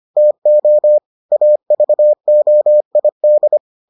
Morse code, heard as a keyed tone, about 25 wpm, 600 Hz. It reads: TO AVOID